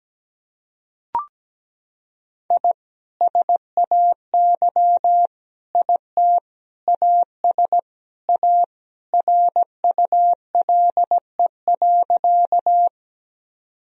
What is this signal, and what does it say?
Morse code, heard as a keyed tone, about 17 wpm, 700 Hz. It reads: I SAY IT AS A RULE.
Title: Sentences from Top 1000 Words - Review Part 2 - 17wpm